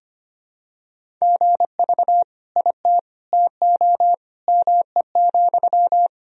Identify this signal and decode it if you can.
Morse code, heard as a keyed tone, about 25 wpm, 700 Hz. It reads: GV IT TO ME,
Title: Sentences from Top 200 Words and Abbreviations 25wpm